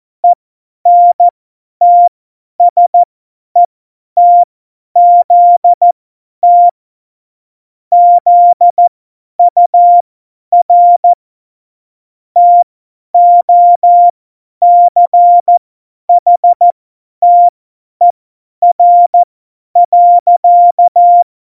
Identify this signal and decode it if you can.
Morse code, heard as a keyed tone, about 14 wpm, spaced as for 10 wpm, 700 Hz. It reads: ENTSETZT ZUR TOCHTER.
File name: Morsekurs / Witze / Witz 27. August 2019, 50 F70